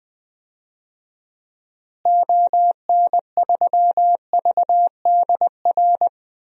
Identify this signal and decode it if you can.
Morse code, heard as a keyed tone, about 20 wpm, 700 Hz. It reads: ON3VDR
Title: DXCC-100 Call Signs - 6 Characters - Repeated 2x - 20wpm